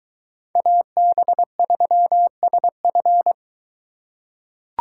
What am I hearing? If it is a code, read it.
Morse code, heard as a keyed tone, about 22 wpm, 700 Hz. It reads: AB3SF